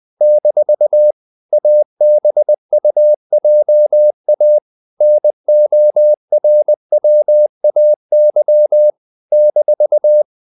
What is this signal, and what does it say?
Morse code, heard as a keyed tone, about 20 wpm, 600 Hz. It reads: - ABUJA NORWAY -
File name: CW / Practice / NationalCapitals20